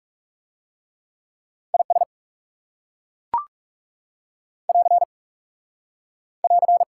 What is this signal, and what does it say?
Morse code, heard as a keyed tone, about 36 wpm, 700 Hz. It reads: IS <AR> <AR>
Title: Speed-Racing Top 180 QSO Elements 30wpm